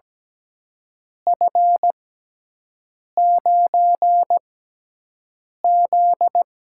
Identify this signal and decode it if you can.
Morse code, heard as a keyed tone, about 17 wpm, spaced as for 6 wpm, 700 Hz. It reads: F9Z
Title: USA Call Signs - 4 Characters - 6x Character Spacing - 17wpm